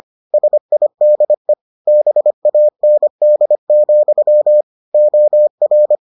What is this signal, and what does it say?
Morse code, heard as a keyed tone, about 25 wpm, 600 Hz. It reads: SIDE BAND, OR